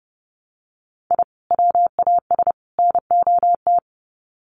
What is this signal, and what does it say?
Morse code, heard as a keyed tone, about 30 wpm, 700 Hz. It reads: I WAS NOT
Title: Sets of 3 Words 30wpm